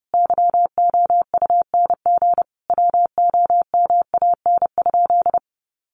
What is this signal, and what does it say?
Morse code, heard as a keyed tone, about 30 wpm, 700 Hz. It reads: YOUNG WOMAN?
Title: Sentences from Top 1100 Words 30wpm